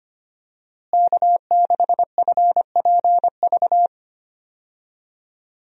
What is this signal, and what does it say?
Morse code, heard as a keyed tone, about 25 wpm, 700 Hz. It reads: K6FPV
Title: USA Call Signs - 5 Characters - Repeated 3x - 25wpm